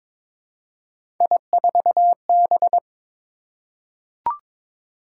Morse code, heard as a keyed tone, about 22 wpm, 700 Hz. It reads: I4B
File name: practice / 22wpm / Three Letter-Number Combinations - 22wpm